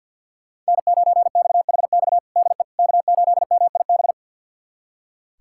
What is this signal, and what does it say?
Morse code, heard as a keyed tone, about 46 wpm, 700 Hz. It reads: N9XHX DE K8MIB
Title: USA Callsign DE Callsign - 5 Characters - 50wpm